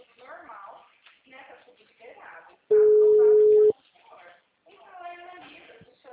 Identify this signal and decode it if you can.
Ringback tone (Europe, 425 Hz), with speech.